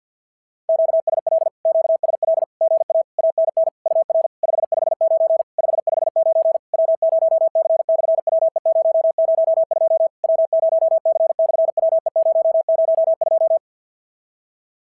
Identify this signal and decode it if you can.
Morse code, heard as a keyed tone, about 54 wpm, 650 Hz. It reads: QSL QSL GA ANN UR 559 559 W0C/WE002 W0C/WE002
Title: SOTA QSO Practice 50wpm